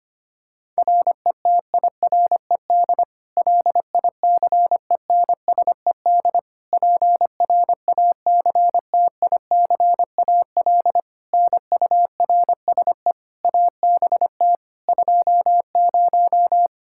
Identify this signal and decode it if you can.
Morse code, heard as a keyed tone, about 25 wpm, 700 Hz. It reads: RETIRED LICENSED PRACTICAL NURSE ABT 20
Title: Practice QSO - Exchange 3 - More Info - 25wpm